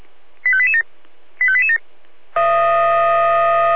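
Five-tone selective call (ZVEI), heard as quick runs of five tones, tones of about 70 ms. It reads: 85887 85887
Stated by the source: ZVEI five-tone selective call received on SDR.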